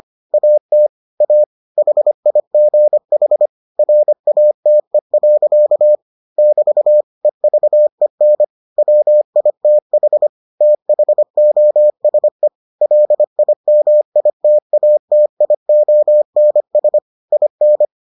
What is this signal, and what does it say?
Morse code, heard as a keyed tone, about 25 wpm, 600 Hz. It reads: AT A HIGH RATE. <BT> EVEN WITH THOSE LIMITATIONS IN